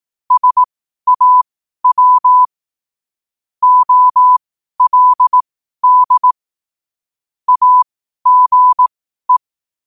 Morse code, heard as a keyed tone, about 18 wpm, spaced as for 12 wpm, 1000 Hz. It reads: SAW OLD AGE
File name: qotdfiles / 12wpm